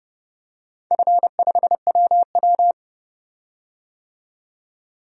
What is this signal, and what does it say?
Morse code, heard as a keyed tone, about 30 wpm, 700 Hz. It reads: F5WW
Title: DXCC-100 Call Signs - 4 Characters - 30wpm